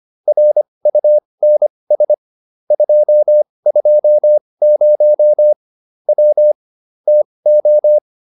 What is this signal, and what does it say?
Morse code, heard as a keyed tone, about 25 wpm, spaced as for 20 wpm, 600 Hz. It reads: RUNS 220 W TO
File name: QSO310a 20